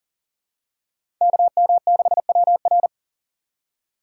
Morse code, heard as a keyed tone, about 40 wpm, 700 Hz. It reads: KM6WR
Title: USA Call Signs - 5 Characters - 3x Character Spacing - 40wpm